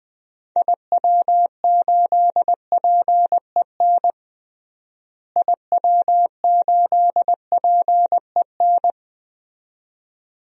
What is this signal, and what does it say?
Morse code, heard as a keyed tone, about 20 wpm, 700 Hz. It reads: IW8PEN IW8PEN
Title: DXCC-100 Call Signs - 6 Characters - Repeated 2x - 20wpm